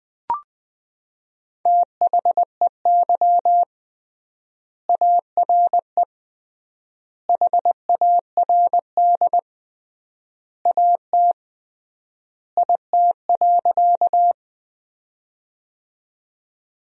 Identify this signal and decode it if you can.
Morse code, heard as a keyed tone, about 20 wpm, 700 Hz. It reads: THEY ARE HARD AT IT.
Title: 3x Word-Spacing - Sentences from Top 300 Words - 20wpm